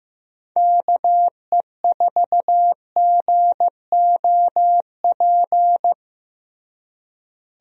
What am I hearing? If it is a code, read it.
Morse code, heard as a keyed tone, about 15 wpm, 700 Hz. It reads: KE4GOP
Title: USA Call Signs - 6 Characters - Repeated 3x - 15wpm